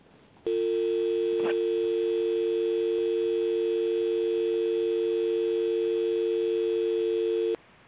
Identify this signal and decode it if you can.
Dial tone (UK), 350 + 450 Hz.